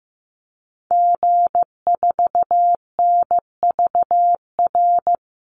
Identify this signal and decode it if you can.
Morse code, heard as a keyed tone, about 15 wpm, 700 Hz. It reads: G4NVR
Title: Call Signs Repeated 3x - 15wpm